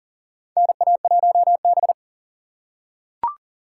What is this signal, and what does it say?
Morse code, heard as a keyed tone, about 40 wpm, 700 Hz. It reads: NA1B
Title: USA Call Signs - 4 Characters - 4x Character Spacing - 40wpm